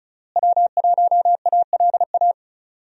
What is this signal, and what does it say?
Morse code, heard as a keyed tone, about 34 wpm, 700 Hz. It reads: W1ALA
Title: USA Call Signs - 5 Characters - Repeated 3x - 35wpm